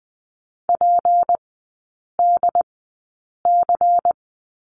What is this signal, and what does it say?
Morse code, heard as a keyed tone, about 20 wpm, spaced as for 8 wpm, 700 Hz. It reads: PDC
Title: 184 Lesson 30e - Letter b - Three Characters 20wpm